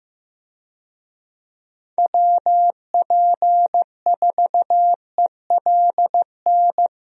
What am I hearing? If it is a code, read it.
Morse code, heard as a keyed tone, about 15 wpm, 700 Hz. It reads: WP4ELN
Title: USA Call Signs - 6 Characters - Repeated 3x - 15wpm